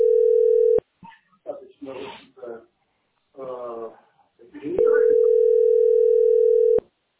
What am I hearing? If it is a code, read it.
Ringback tone (North America), 440 + 480 Hz, with speech.
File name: 3ddc-4cb5-87e9